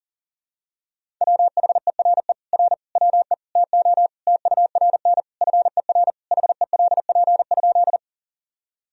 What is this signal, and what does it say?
Morse code, heard as a keyed tone, about 37 wpm, 700 Hz. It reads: WHERE R WE TO TURN FER HELP?
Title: Sentences from Top 200 Words and Abbreviations 40wpm